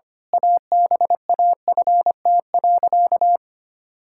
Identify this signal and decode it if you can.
Morse code, heard as a keyed tone, about 25 wpm, 700 Hz. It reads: ABAFT.
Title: Pangram Challenge v2 - 25wpm